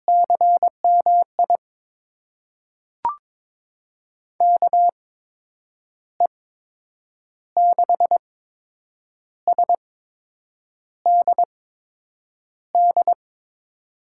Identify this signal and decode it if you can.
Morse code, heard as a keyed tone, about 21 wpm, spaced as for 8 wpm, 700 Hz. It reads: CMI KE6SDD
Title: USA Call Signs - 6 Characters - 8x Character Spacing - 22wpm